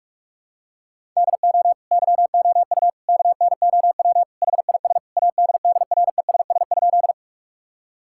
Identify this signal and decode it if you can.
Morse code, heard as a keyed tone, about 47 wpm, 700 Hz. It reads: DO YOU KNOW HIS ADDRESS?
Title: Sentences from Top 1000 Words - Review Part 6 - 45wpm